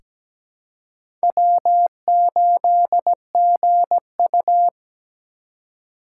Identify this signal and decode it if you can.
Morse code, heard as a keyed tone, about 17 wpm, 700 Hz. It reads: W8GU
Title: K1USN SST QSO Practice - 17wpm